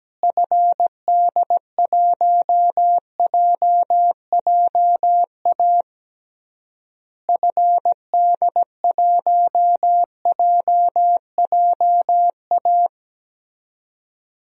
Morse code, heard as a keyed tone, about 17 wpm, 700 Hz. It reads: FD1JJA FD1JJA
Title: DXCC-100 Call Signs - 6 Characters - Repeated 3x - 17wpm